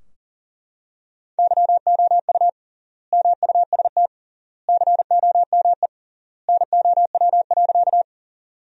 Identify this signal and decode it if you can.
Morse code, heard as a keyed tone, about 37 wpm, 700 Hz. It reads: YOU MUST COME NOW.
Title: 3x Word-Spacing - Sentences from Top 200 Words - 40wpm